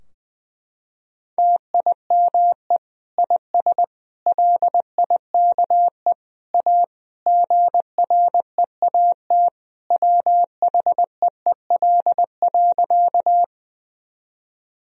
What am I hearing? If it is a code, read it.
Morse code, heard as a keyed tone, about 20 wpm, 700 Hz. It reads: TIME IS LIKE A GREAT WHEEL.
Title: Sentences from Top 1000 Words - Review Part 3 - 20wpm